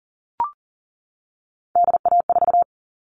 Morse code, heard as a keyed tone, about 39 wpm, 700 Hz. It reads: DA4
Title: All Cut Numbers - 3-Digit 40wpm